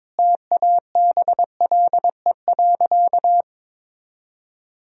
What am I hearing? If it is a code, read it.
Morse code, heard as a keyed tone, about 22 wpm, 700 Hz. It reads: TABLE.
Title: Sentences from Top 1000 Words - Review Part 4 - 22wpm